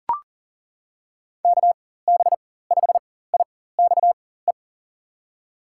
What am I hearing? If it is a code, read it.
Morse code, heard as a keyed tone, about 37 wpm, spaced as for 18 wpm, 700 Hz. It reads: KB5IXE
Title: USA Call Signs - 6 Characters - 4x Character Spacing - 40wpm